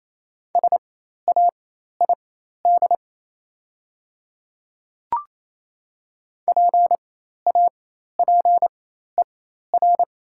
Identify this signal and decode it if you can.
Morse code, heard as a keyed tone, about 28 wpm, spaced as for 13 wpm, 700 Hz. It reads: SAID PAPER
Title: 4x Character Spacing - Top 300 Words - 28wpm